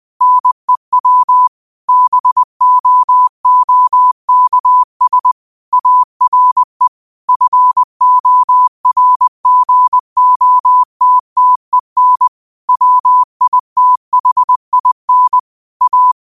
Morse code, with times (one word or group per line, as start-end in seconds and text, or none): 0.2-1.5 NEW
1.9-5.3 BOOKS
5.7-6.9 ARE
7.3-12.3 FORGOTTEN
12.7-15.4 WITHIN
15.8-16.1 A